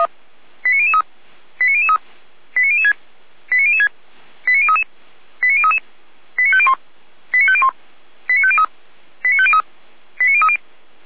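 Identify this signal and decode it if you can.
Five-tone selective call (ZVEI), heard as quick runs of five tones, tones of about 70 ms. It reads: 89003 89003 89006 89006 89033 89033 89501 89501 89503 89503 89030
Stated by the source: ZVEI five-tone selective call received by SDR, fire brigade (Feuerwehr) Orlamünde.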